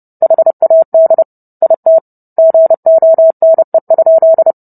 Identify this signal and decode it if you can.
Morse code, heard as a keyed tone, about 30 wpm, 650 Hz. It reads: HAD IT GONE?
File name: Book Two / 04-Chapter 4 - 30 WPM